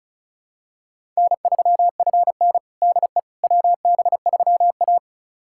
Morse code, heard as a keyed tone, about 36 wpm, 700 Hz. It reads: N3FN DE WB3A
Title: USA Callsign DE Callsign - 4 Characters - 35wpm